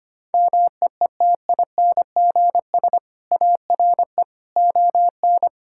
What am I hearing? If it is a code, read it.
Morse code, heard as a keyed tone, about 25 wpm, 700 Hz. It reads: MEETINGS ARE ON
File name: Easy Sentences with Hints - 25wpm